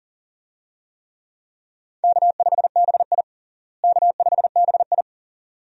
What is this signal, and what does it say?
Morse code, heard as a keyed tone, about 40 wpm, 700 Hz. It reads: K5BI K5BI
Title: USA Call Signs - 4 Characters - Repeated 2x - 40wpm